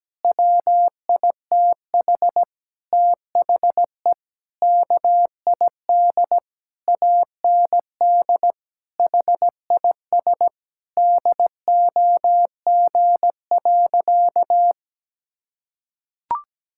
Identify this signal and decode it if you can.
Morse code, heard as a keyed tone, about 17 wpm, 700 Hz. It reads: WITH THE KID AND HIS DOG.
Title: Sentences from Top 1600 Words 17wpm